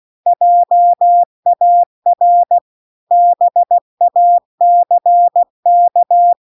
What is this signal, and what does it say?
Morse code, heard as a keyed tone, about 16 wpm, 700 Hz. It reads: JAR BACK